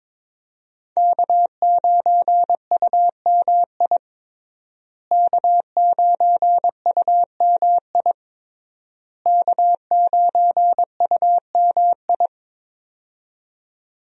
Morse code, heard as a keyed tone, about 21 wpm, 700 Hz. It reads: K9UMI K9UMI K9UMI
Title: USA Call Signs - 5 Characters - Repeated 3x - 22wpm